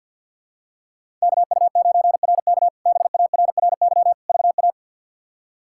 Morse code, heard as a keyed tone, about 51 wpm, 700 Hz. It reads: KU9RK BARRY VA
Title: K1USN SST QSO Practice - 50wpm